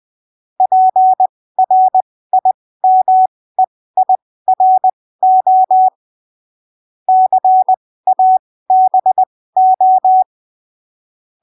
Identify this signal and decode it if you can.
Morse code, heard as a keyed tone, about 20 wpm, spaced as for 15 wpm, 750 Hz. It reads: PRIMEIRO CABO
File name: arquivos / texto / texto 16 20ppm